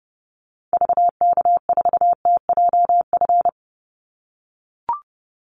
Morse code, heard as a keyed tone, about 30 wpm, 700 Hz. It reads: VK4TJF
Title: Call Signs - 30wpm